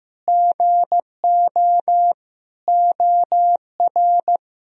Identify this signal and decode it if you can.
Morse code, heard as a keyed tone, about 15 wpm, 700 Hz. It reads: GO OR
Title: Sentences from Top 1000 Words - Review Part 3 - 15wpm